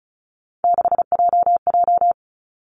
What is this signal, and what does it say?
Morse code, heard as a keyed tone, about 36 wpm, 700 Hz. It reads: 6JJ
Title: Three Challenging Letter-Number Combinations 35wpm